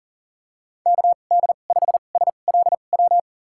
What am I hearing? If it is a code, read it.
Morse code, heard as a keyed tone, about 38 wpm, spaced as for 28 wpm, 700 Hz. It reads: KD5SLW
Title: USA Call Signs - 6 Characters - 2x Character Spacing - 40wpm